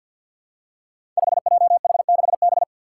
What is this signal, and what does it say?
Morse code, heard as a keyed tone, about 47 wpm, 700 Hz. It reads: 5JH6B
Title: Five Challenging Letter-Number Combinations 50wpm